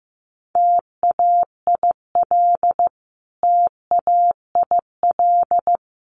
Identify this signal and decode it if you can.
Morse code, heard as a keyed tone, about 15 wpm, 700 Hz. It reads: TAIL TAIL
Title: Top 500 Words Repeated 15wpm